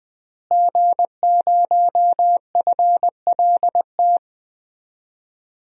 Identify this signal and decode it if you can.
Morse code, heard as a keyed tone, about 20 wpm, 700 Hz. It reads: G0FLT
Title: DXCC-100 Call Signs - 5 Characters - Repeated 2x - 20wpm